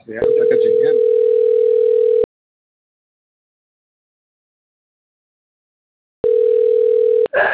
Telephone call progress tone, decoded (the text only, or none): ringback tone